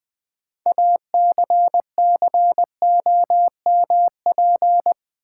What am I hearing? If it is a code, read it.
Morse code, heard as a keyed tone, about 20 wpm, 700 Hz. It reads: ACCOMP
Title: Build A Word - 3 Letters At A Time - 9-Letter Words - 20wpm